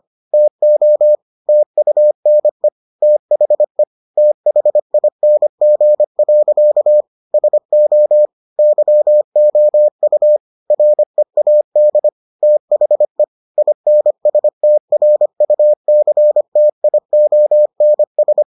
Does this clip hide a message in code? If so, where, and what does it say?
Yes: Morse code from 0.3 to 18.4 s, heard as a keyed tone, about 25 wpm, 600 Hz. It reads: TO TUNE THE THING. SO YOU READ THE INSTRUCTIONS